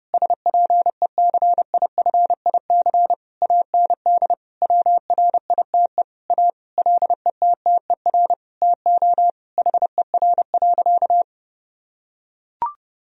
Morse code, heard as a keyed tone, about 30 wpm, 700 Hz. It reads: SPECIFIC AND WRITE A LETTER TO HER.
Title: Sentences from Top 1000 Words - Review Part 3 - 30wpm